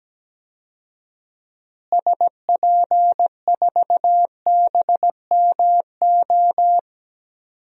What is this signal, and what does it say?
Morse code, heard as a keyed tone, about 17 wpm, 700 Hz. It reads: SP4BMO
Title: DXCC-100 Call Signs - 6 Characters - Repeated 3x - 17wpm